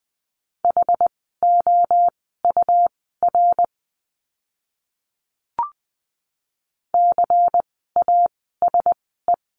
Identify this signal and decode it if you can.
Morse code, heard as a keyed tone, about 20 wpm, spaced as for 15 wpm, 700 Hz. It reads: HOUR CASE
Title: 108 Lesson 18f - Letter m - Words 20wpm